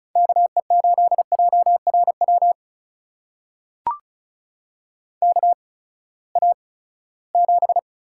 Morse code, heard as a keyed tone, about 36 wpm, spaced as for 20 wpm, 700 Hz. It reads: KE8JRW KA7
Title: USA Call Signs - 6 Characters - 8x Character Spacing - 35wpm